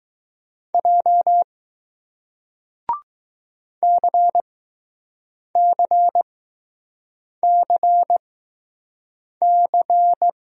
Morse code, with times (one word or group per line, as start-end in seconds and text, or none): 0.7-1.4 J
3.8-10.3 CCCC